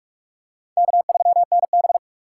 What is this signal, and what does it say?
Morse code, heard as a keyed tone, about 44 wpm, 690 Hz. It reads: K3NB
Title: USA Call Signs - 4 Characters - 4x Character Spacing - 45wpm